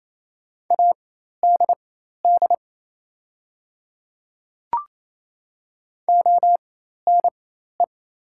Morse code, heard as a keyed tone, about 28 wpm, spaced as for 13 wpm, 700 Hz. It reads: ADD ONE
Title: 4x Character Spacing - Top 500 Words - 28wpm